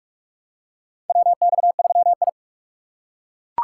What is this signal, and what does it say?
Morse code, heard as a keyed tone, about 42 wpm, 700 Hz. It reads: WX3I
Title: USA Call Signs - 4 Characters - 3x Character Spacing - 45wpm